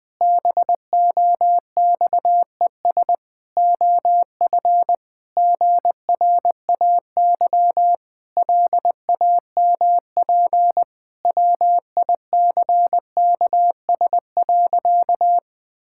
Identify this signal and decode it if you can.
Morse code, heard as a keyed tone, about 19 wpm, 700 Hz. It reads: BOXES OF GRAY LAMP WICKS.
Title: Pangram Challenge v2 - 20wpm